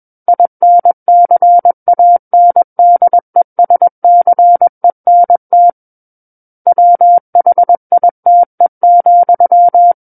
Morse code, heard as a keyed tone, about 21 wpm, 700 Hz. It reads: INCANDESCENT WHITE,
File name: Book One / 12-Chapter 12 - 21 WPM